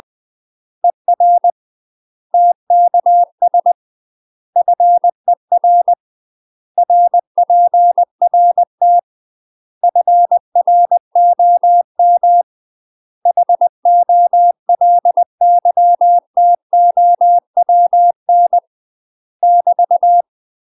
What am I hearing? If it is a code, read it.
Morse code, heard as a keyed tone, about 20 wpm, 700 Hz. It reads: ER TKS FER RPRT FROM HOLYTOWN <BT>
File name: UK MASTER QSOs TRAINING/20WPM